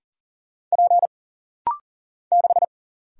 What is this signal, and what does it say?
Morse code, heard as a keyed tone, about 40 wpm, 700 Hz. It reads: P 6
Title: Single Letter-Number - Rapid-Fire 40wpm